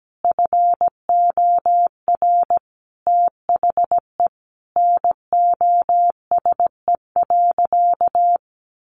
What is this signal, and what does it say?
Morse code, heard as a keyed tone, about 17 wpm, 700 Hz. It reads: FOR THE NOSE.
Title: Easy Phrases 17wpm